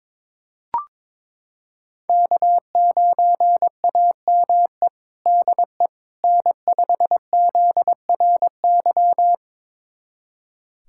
Morse code, heard as a keyed tone, about 21 wpm, 700 Hz. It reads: K9AME DE N5ZRY
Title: USA Callsign DE Callsign - 5 Characters - 22wpm